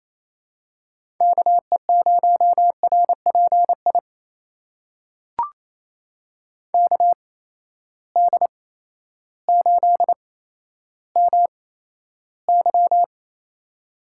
Morse code, heard as a keyed tone, about 28 wpm, spaced as for 13 wpm, 700 Hz. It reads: KE0RPI KD8MY
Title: USA Call Signs - 8x Character Spacing - 28wpm